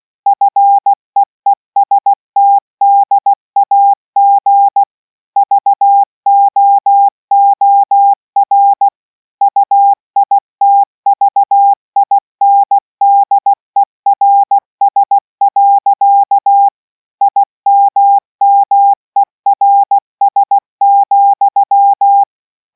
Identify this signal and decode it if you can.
Morse code, heard as a keyed tone, about 16 wpm, 800 Hz. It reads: FEESTDAG VOOR UITVINDERS. IMMERS,